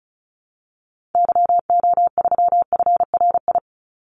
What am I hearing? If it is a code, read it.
Morse code, heard as a keyed tone, about 34 wpm, 700 Hz. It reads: YO3FRI
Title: Call Signs Repeated 2x - 35wpm